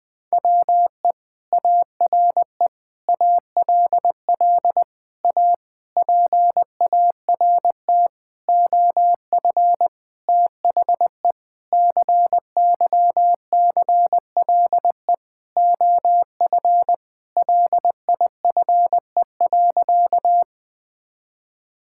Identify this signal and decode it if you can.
Morse code, heard as a keyed tone, about 20 wpm, 700 Hz. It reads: WE ARE ALL A PART OF THE CYCLE OF LIFE.
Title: Sentences from Top 1000 Words - Review Part 6 - 20wpm